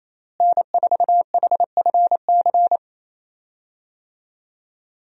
Morse code, heard as a keyed tone, about 28 wpm, 700 Hz. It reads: N4HFC
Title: USA Call Signs - 5 Characters - Repeated 3x - 28wpm